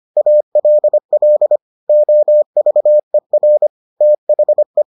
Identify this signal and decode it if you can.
Morse code, heard as a keyed tone, about 25 wpm, 600 Hz. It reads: ALL OVER THE